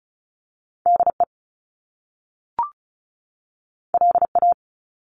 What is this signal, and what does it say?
Morse code, heard as a keyed tone, about 36 wpm, 700 Hz. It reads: DE LA
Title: Top Two Letter Combinations in Words 35wpm